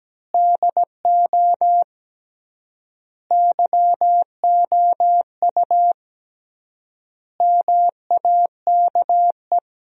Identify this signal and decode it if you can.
Morse code, heard as a keyed tone, about 17 wpm, 700 Hz. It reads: DO YOU MAKE
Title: 3x Word-Spacing - Sentences from Top 100 Words - 17wpm